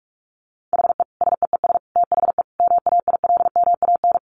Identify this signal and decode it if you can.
Morse code, heard as a keyed tone, about 44 wpm, 700 Hz. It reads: HE SEES THE MAILMAN
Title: Additional Easy Phrases 45wpm